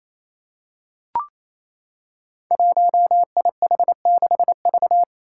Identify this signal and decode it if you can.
Morse code, heard as a keyed tone, about 28 wpm, 700 Hz. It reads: 1IH6V
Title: Five Challenging Letter-Number Combinations 28wpm